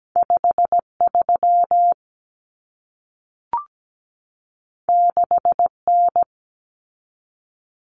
Morse code, heard as a keyed tone, about 17 wpm, 700 Hz. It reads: 53 6N
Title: Common Cut Numbers - 2-Digit 17wpm